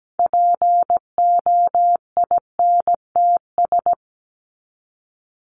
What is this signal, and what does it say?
Morse code, heard as a keyed tone, about 17 wpm, 700 Hz. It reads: POINTS
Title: ES:S Suffix - Top 100 Words with Base Words 17wpm